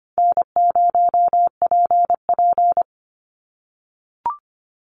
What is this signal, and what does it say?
Morse code, heard as a keyed tone, about 25 wpm, 700 Hz. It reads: N0PP
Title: Call Signs - 25wpm